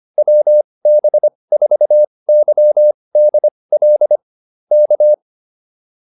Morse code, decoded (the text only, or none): WB4YDL K